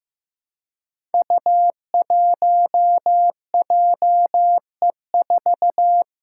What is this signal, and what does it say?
Morse code, heard as a keyed tone, about 15 wpm, 700 Hz. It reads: U1JE4
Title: Five Challenging Letter-Number Combinations 15wpm